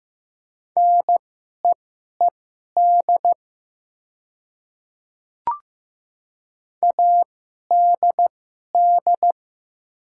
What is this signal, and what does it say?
Morse code, heard as a keyed tone, about 15 wpm, spaced as for 11 wpm, 700 Hz. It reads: NEED ADD
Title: 2x Character Spacing - Top 200 Words - 15wpm